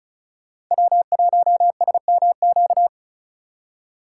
Morse code, heard as a keyed tone, about 45 wpm, 700 Hz. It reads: W1SMQ